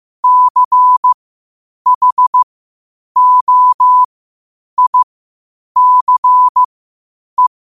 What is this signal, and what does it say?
Morse code, heard as a keyed tone, about 15 wpm, spaced as for 8 wpm, 1000 Hz. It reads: CHOICE
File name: qotdfiles / 8wpm